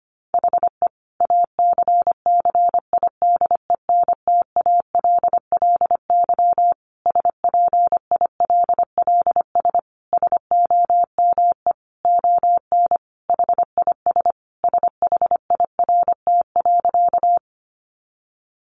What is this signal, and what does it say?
Morse code, heard as a keyed tone, about 25 wpm, 700 Hz. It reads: HE ACCIDENTALLY SPILLS SOME ON HIS SHIRT.
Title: Additional Easy Phrases 25wpm